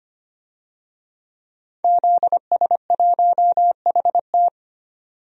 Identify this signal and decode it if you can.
Morse code, heard as a keyed tone, about 25 wpm, 700 Hz. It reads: ZS1HT
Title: DXCC-100 Call Signs - 5 Characters - 25wpm